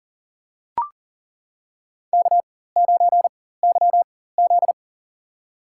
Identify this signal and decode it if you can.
Morse code, heard as a keyed tone, about 40 wpm, spaced as for 19 wpm, 700 Hz. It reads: K9YZ
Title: USA Call Signs - 4 Characters - 4x Character Spacing - 40wpm